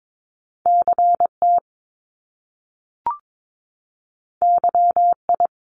Morse code, heard as a keyed tone, about 22 wpm, 700 Hz. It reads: CT YI
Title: Top Two Letter Combinations in Words 22wpm